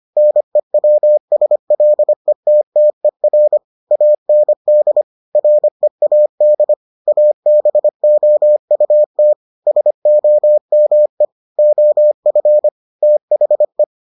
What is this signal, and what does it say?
Morse code, heard as a keyed tone, about 25 wpm, 600 Hz. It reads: NEWSLETTER AND READ ABOUT SOME OF THE